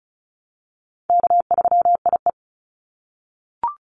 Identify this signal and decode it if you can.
Morse code, heard as a keyed tone, about 36 wpm, 700 Hz. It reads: K3IE
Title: Call Signs - 35wpm